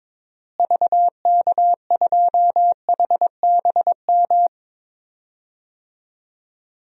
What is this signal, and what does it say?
Morse code, heard as a keyed tone, about 21 wpm, 700 Hz. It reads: VK2HBM
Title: DXCC-100 Call Signs - 6 Characters - Repeated 3x - 22wpm